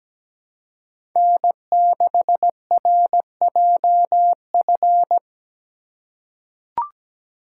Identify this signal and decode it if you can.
Morse code, heard as a keyed tone, about 17 wpm, 700 Hz. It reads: N6RJF
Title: USA Call Signs - 2x Character Spacing - 17wpm